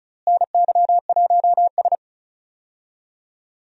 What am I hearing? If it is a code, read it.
Morse code, heard as a keyed tone, about 36 wpm, 700 Hz. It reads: NY1S
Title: USA Call Signs - 4 Characters - Repeated 3x - 35wpm